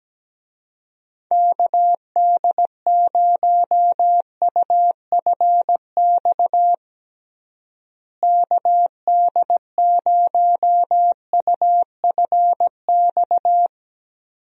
Morse code, heard as a keyed tone, about 17 wpm, 700 Hz. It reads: KD0UFX KD0UFX
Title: USA Call Signs - 6 Characters - Repeated 2x - 17wpm